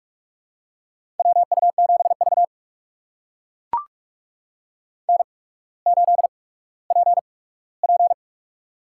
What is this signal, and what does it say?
Morse code, heard as a keyed tone, about 44 wpm, spaced as for 19 wpm, 690 Hz. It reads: WU7V N8PP
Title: USA Call Signs - 4 Characters - 8x Character Spacing - 45wpm